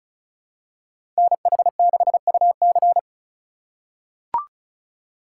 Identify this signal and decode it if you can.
Morse code, heard as a keyed tone, about 36 wpm, 700 Hz. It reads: NH6UC
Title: USA Call Signs - 4x Character Spacing - 35wpm